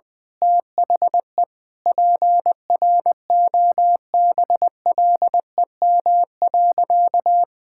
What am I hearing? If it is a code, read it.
Morse code, heard as a keyed tone, about 20 wpm, 700 Hz. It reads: THE PROBLEM.
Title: Sentences with 9-Word Phrases - Spoken Beginning - 20wpm